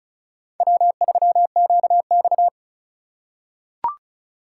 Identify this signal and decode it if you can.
Morse code, heard as a keyed tone, about 36 wpm, 700 Hz. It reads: W3QX
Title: USA Call Signs - 4 Characters - 3x Character Spacing - 35wpm